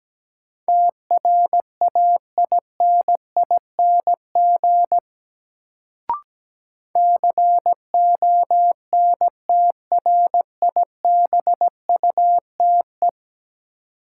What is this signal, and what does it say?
Morse code, heard as a keyed tone, about 17 wpm, 700 Hz. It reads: TRAINING CONTRIBUTE
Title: Sentences from Top 1400 Words - New Word Drill Alt - 17wpm